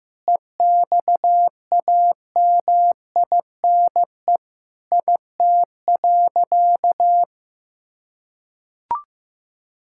Morse code, heard as a keyed tone, about 16 wpm, 700 Hz. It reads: EXAMINE IT.
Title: Sentences from Top 1600 Words 15wpm